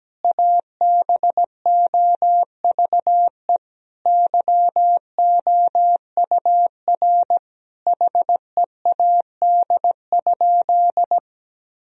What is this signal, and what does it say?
Morse code, heard as a keyed tone, about 17 wpm, 700 Hz. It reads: ABOVE YOUR HEAD?
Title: Sentences from Top 1000 Words - Review Part 3 - 17wpm